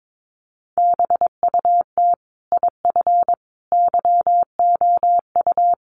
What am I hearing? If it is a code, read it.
Morse code, heard as a keyed tone, about 22 wpm, 700 Hz. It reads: BUT IF YOU
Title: Sentences from Top 1200 Words 22wpm